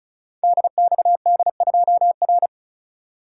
Morse code, heard as a keyed tone, about 36 wpm, 700 Hz. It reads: DXD2R
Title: Code Groups - 5 Letter:Numbers - 35wpm